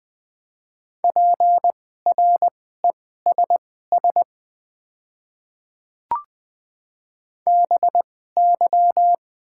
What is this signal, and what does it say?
Morse code, heard as a keyed tone, about 20 wpm, spaced as for 14 wpm, 700 Hz. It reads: PRESS BY
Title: 2x Character Spacing - Top 300 Words - 20wpm